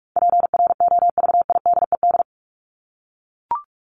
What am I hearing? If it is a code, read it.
Morse code, heard as a keyed tone, about 44 wpm, 700 Hz. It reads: PROVIDED
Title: Top 1751-2000 Words 45wpm